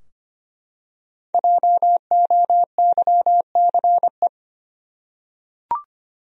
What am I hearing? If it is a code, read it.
Morse code, heard as a keyed tone, about 25 wpm, 700 Hz. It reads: JOYCE
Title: USA Common Names - Max 5 Letters - 25wpm